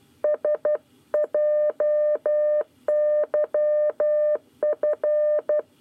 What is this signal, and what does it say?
Morse code, heard as a keyed tone, about 12 wpm, 600 Hz. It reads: SJYF
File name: 9 lezione cw 1